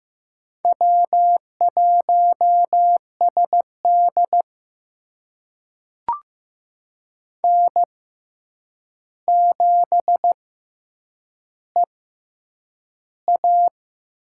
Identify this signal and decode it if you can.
Morse code, heard as a keyed tone, about 15 wpm, spaced as for 8 wpm, 700 Hz. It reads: W1SD N7EA
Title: USA Call Signs - 4 Characters - 6x Character Spacing - 15wpm